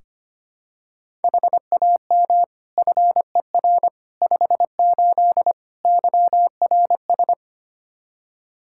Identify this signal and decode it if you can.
Morse code, heard as a keyed tone, about 25 wpm, 700 Hz. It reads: HAM FER 58 YRS
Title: Practice QSO - Exchange 3 - More Info - 25wpm